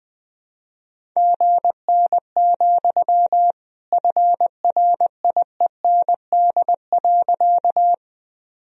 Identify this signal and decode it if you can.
Morse code, heard as a keyed tone, about 21 wpm, 700 Hz. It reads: GN, FRIEND.
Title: Sentences from Top 300 Words and Abbreviations 20wpm